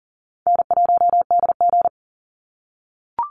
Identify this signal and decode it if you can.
Morse code, heard as a keyed tone, about 38 wpm, 700 Hz. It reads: N1DG
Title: Easy Call Signs - 40wpm